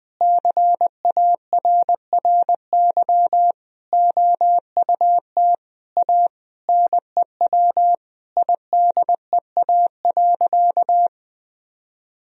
Morse code, heard as a keyed tone, about 20 wpm, 700 Hz. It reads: CARRY OUT A NEW IDEA.
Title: Sentences from Top 1000 Words - Review Part 1 - 20wpm